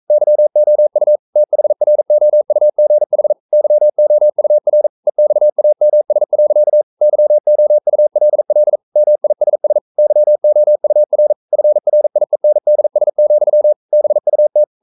Morse code, heard as a keyed tone, about 42 wpm, 600 Hz. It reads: YOU THROUGH YOUR EXAMS. YOULL MISS YOUR FRIENDS, BUT
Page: Romanen Never af Ken Follet på engelsk.